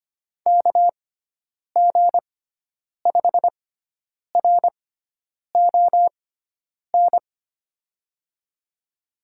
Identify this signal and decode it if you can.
Morse code, heard as a keyed tone, about 25 wpm, spaced as for 9 wpm, 700 Hz. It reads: KG5RON
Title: USA Call Signs - 6x Character Spacing - 25wpm